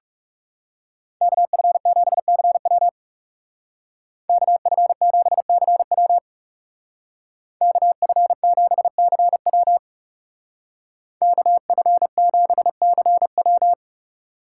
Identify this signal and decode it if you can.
Morse code, heard as a keyed tone, about 37 wpm, 700 Hz. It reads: KF7CW KF7CW KF7CW KF7CW
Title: Speed Racing - USA 2x2 Call Signs 30wpm